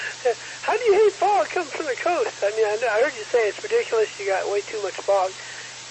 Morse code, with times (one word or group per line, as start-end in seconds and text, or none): none